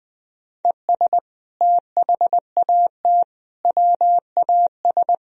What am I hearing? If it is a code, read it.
Morse code, heard as a keyed tone, about 21 wpm, 700 Hz. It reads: ES THAT WAS
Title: Sentences from Top 200 Words and Abbreviations 20wpm